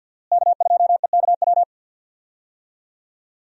Morse code, heard as a keyed tone, about 50 wpm, 700 Hz. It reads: K2EKW
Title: USA Call Signs - 5 Characters - Repeated 3x - 50wpm